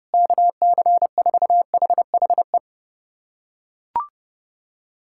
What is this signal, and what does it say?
Morse code, heard as a keyed tone, about 32 wpm, 700 Hz. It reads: KC4HHE
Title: USA Callsign DE Callsign 30wpm